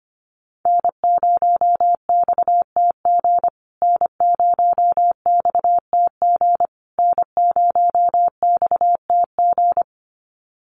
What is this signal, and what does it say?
Morse code, heard as a keyed tone, about 25 wpm, 700 Hz. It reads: N0XTG N0XTG N0XTG
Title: USA Call Signs Repeated 3x 25wpm